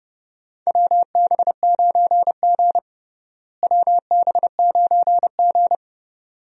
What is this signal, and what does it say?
Morse code, heard as a keyed tone, about 29 wpm, 700 Hz. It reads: WB9G WB9G
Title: USA Call Signs - 4 Characters - Repeated 2x - 30wpm